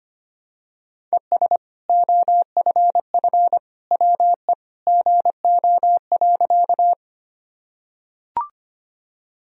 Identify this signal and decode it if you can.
Morse code, heard as a keyed tone, about 25 wpm, 700 Hz. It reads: ES OFF WE GO.
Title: Sentences from Top 200 Words and Abbreviations 25wpm